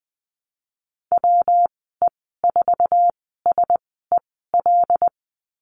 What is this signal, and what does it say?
Morse code, heard as a keyed tone, about 19 wpm, spaced as for 14 wpm, 700 Hz. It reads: WE4SEL
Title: 122 Lesson 20g - Number 3 - Call Signs 20wpm